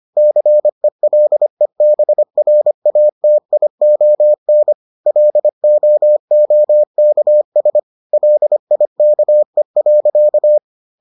Morse code, heard as a keyed tone, about 25 wpm, 600 Hz. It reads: CELEBRATION LOOKS LIKE.